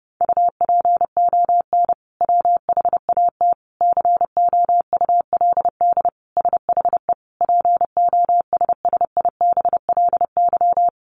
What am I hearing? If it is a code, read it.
Morse code, heard as a keyed tone, about 30 wpm, 700 Hz. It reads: UPON WHAT COULD SHE POSSIBLY
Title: Sentences from Top 1500 Words 30wpm